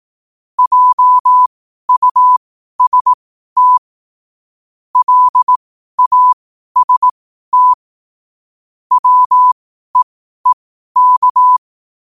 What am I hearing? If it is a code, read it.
Morse code, heard as a keyed tone, about 18 wpm, spaced as for 12 wpm, 1000 Hz. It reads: JUST LAST WEEK
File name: qotdfiles / 12wpm